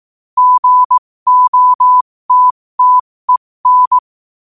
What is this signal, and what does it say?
Morse code, heard as a keyed tone, about 18 wpm, spaced as for 15 wpm, 1000 Hz. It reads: GOTTEN
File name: 15wpm